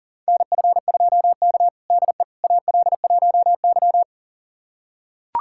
Morse code, heard as a keyed tone, about 40 wpm, 700 Hz. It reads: NF2K DE AL1Y
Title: USA Callsign DE Callsign - 4 Characters - 40wpm